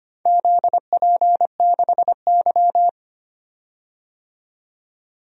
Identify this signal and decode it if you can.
Morse code, heard as a keyed tone, about 25 wpm, 700 Hz. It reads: ZP6Y
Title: DXCC-100 Call Signs - 4 Characters - 25wpm